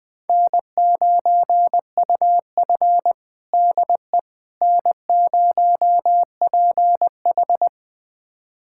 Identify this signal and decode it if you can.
Morse code, heard as a keyed tone, about 20 wpm, 700 Hz. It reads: N9UF DE N0PH
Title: USA Callsign DE Callsign - 4 Characters - 20wpm